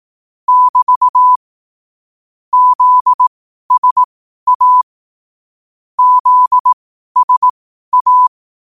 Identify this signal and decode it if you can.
Morse code, heard as a keyed tone, about 18 wpm, spaced as for 12 wpm, 1000 Hz. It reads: <BT> ZSA ZSA
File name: qotdfiles / 12wpm